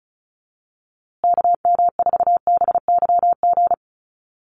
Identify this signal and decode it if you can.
Morse code, heard as a keyed tone, about 34 wpm, 700 Hz. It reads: KM4BYG
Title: USA Call Signs Repeated 3x 35wpm